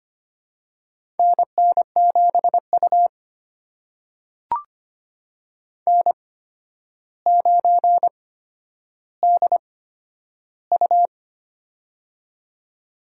Morse code, heard as a keyed tone, about 25 wpm, spaced as for 11 wpm, 700 Hz. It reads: NN7U N9DU
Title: USA Call Signs - 4 Characters - 8x Character Spacing - 25wpm